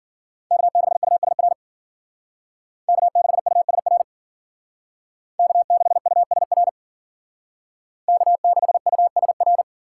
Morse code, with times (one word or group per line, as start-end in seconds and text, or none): 0.5-1.5 K6USR
2.9-4.0 K6USR
5.4-6.7 K6USR
8.1-9.6 K6USR